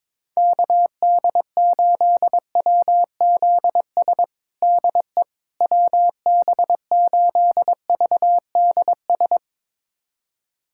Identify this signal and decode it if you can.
Morse code, heard as a keyed tone, about 22 wpm, 700 Hz. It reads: KD8WZS DE WB8VDS
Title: USA Callsign DE Callsign - 6 Characters - 22wpm